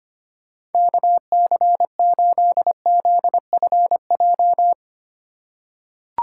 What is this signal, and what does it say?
Morse code, heard as a keyed tone, about 25 wpm, 700 Hz. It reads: KC8ZFJ
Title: USA Call Signs - 6 Characters - 2x Character Spacing - 25wpm